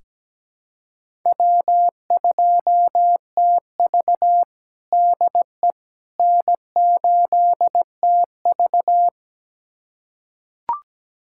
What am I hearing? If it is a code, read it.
Morse code, heard as a keyed tone, about 17 wpm, 700 Hz. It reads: W2TV DE N8TV
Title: USA Callsign DE Callsign - 4 Characters - 17wpm